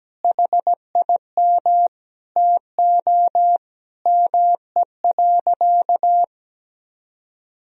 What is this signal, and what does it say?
Morse code, heard as a keyed tone, about 17 wpm, 700 Hz. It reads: HIM TO ME.
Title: Sentences from Top 200 Words and Abbreviations 17wpm